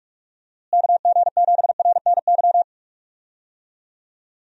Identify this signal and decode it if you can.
Morse code, heard as a keyed tone, about 46 wpm, 700 Hz. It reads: KG7RNY
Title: USA Call Signs - 6 Characters - Repeated 3x - 45wpm